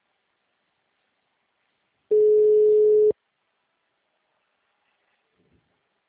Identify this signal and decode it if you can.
Ringback tone (Europe, 425 Hz).